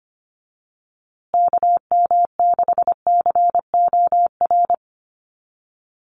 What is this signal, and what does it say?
Morse code, heard as a keyed tone, about 25 wpm, 700 Hz. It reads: KM6COR
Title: USA Call Signs Repeated 3x 25wpm